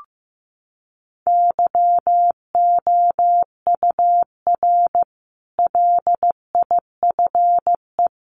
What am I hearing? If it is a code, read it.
Morse code, heard as a keyed tone, about 15 wpm, 700 Hz. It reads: YOUR LIFE
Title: Sentences from Top 1300 Words 15wpm